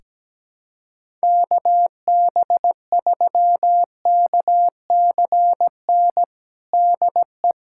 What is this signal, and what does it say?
Morse code, heard as a keyed tone, about 17 wpm, 700 Hz. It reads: KB3KCN DE
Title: USA Callsign DE Callsign - 6 Characters - 17wpm